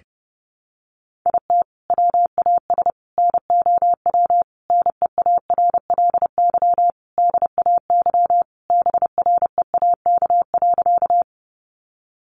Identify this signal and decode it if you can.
Morse code, heard as a keyed tone, about 30 wpm, 700 Hz. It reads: IT WAS NOW NEARLY DAYBREAK.
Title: Sentences from Top 1300 Words 30wpm